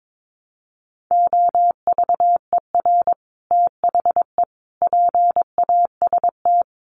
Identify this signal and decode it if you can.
Morse code, heard as a keyed tone, about 22 wpm, 700 Hz. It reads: OVER THE PAST
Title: Sets of 3 Words 22wpm